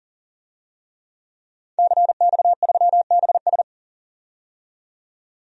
Morse code, heard as a keyed tone, about 40 wpm, 700 Hz. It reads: CX3BS
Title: DXCC-100 Call Signs - 5 Characters - 40wpm